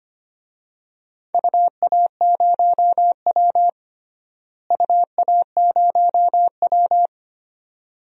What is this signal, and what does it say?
Morse code, heard as a keyed tone, about 25 wpm, 700 Hz. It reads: UA0W UA0W
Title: DXCC-100 Call Signs - 4 Characters - Repeated 2x - 25wpm